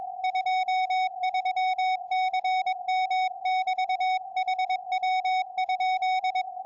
Morse code, heard as a keyed tone, about 22 wpm, 750 Hz, with radio noise. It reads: 23CM <BT> HW?
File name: tropo / sm7gvf 70cm